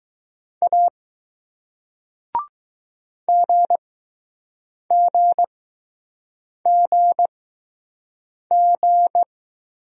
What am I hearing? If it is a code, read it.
Morse code, heard as a keyed tone, about 20 wpm, spaced as for 6 wpm, 700 Hz. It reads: A GGGG